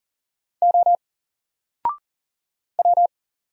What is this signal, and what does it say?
Morse code, heard as a keyed tone, about 40 wpm, 700 Hz. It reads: O W
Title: Single Letter-Number 40wpm